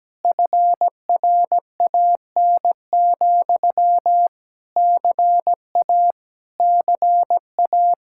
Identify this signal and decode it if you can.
Morse code, heard as a keyed tone, about 17 wpm, 700 Hz. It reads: FRAN, CA CA
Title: Speed-Racing Common QSO Phrases - 17wpm